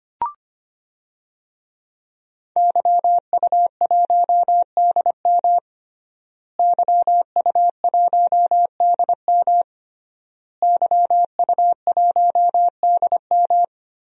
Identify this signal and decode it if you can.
Morse code, heard as a keyed tone, about 25 wpm, 700 Hz. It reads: YU1DM YU1DM YU1DM
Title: DXCC-100 Call Signs - 5 Characters - Repeated 3x - 25wpm